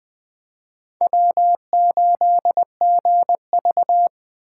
Morse code, heard as a keyed tone, about 20 wpm, 700 Hz. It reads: W8GV
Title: Speed Racing - USA Call Signs 17wpm